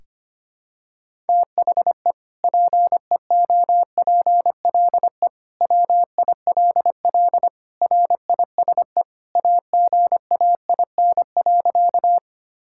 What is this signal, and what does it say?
Morse code, heard as a keyed tone, about 25 wpm, 700 Hz. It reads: THE PEOPLE WILL RISE AGAIN.
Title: Sentences from Top 1000 Words - Review Part 6 - 25wpm